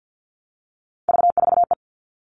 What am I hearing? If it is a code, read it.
Morse code, heard as a keyed tone, about 50 wpm, 700 Hz. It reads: V4E
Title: Three Challenging Letter-Number Combinations 50wpm